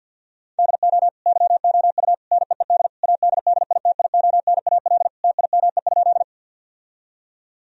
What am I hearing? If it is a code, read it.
Morse code, heard as a keyed tone, about 50 wpm, 700 Hz. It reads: DO YOU NEED ADDITIONAL TIME?
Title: Sentences from Top 1000 Words - Review Part 4 - 50wpm